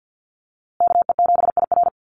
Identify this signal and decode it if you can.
Morse code, heard as a keyed tone, about 50 wpm, 700 Hz. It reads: KE7IR
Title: USA Call Signs Repeated 2x 50wpm